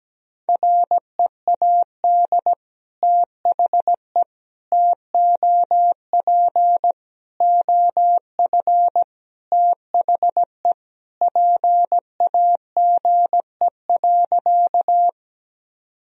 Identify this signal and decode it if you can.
Morse code, heard as a keyed tone, about 17 wpm, 700 Hz. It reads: READ THE TOP OF THE PAGE.
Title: Sentences from Top 1000 Words - Review Part 2 - 17wpm